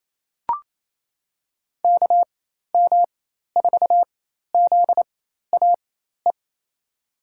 Morse code, heard as a keyed tone, about 28 wpm, spaced as for 13 wpm, 700 Hz. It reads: KM4ZAE
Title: USA Call Signs - 6 Characters - 4x Character Spacing - 28wpm